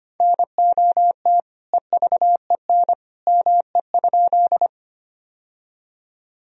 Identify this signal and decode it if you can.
Morse code, heard as a keyed tone, about 25 wpm, 700 Hz. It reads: NOT EVEN ME?
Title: Sentences from Top 1000 Words - Review Part 1 - 25wpm